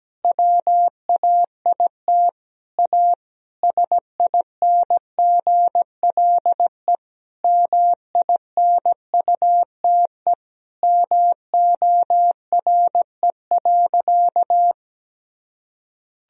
Morse code, heard as a keyed tone, about 17 wpm, 700 Hz. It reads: WAIT A SINGLE MINUTE MORE.
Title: Sentences from Top 1000 Words - Review Part 4 - 17wpm